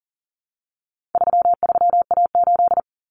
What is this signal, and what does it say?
Morse code, heard as a keyed tone, about 40 wpm, 700 Hz. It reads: 33A8
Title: Common Cut Numbers - 4-Digit 40wpm